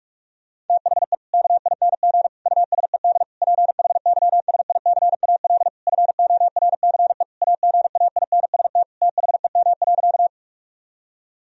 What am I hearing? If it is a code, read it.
Morse code, heard as a keyed tone, about 41 wpm, 700 Hz. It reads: THE KING USED PHYSICAL FORCE AGAINST THEM.
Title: Sentences from Top 1000 Words - Review Part 4 - 45wpm